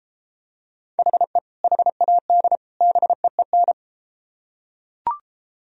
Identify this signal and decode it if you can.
Morse code, heard as a keyed tone, about 32 wpm, 700 Hz. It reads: HE HAD BEEN